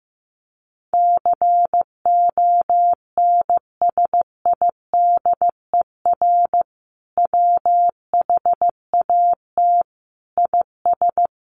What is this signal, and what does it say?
Morse code, heard as a keyed tone, about 15 wpm, 700 Hz. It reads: CONSIDER WHAT IS
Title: Sentences from Top 800 Words 15wpm